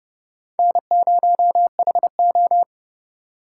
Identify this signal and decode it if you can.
Morse code, heard as a keyed tone, about 28 wpm, 700 Hz. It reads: N0HO